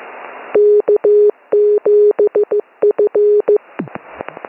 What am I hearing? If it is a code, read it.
Morse code, heard as a keyed tone, about 15 wpm, 420 Hz. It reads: K7F
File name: n6tr / K7FL Beacons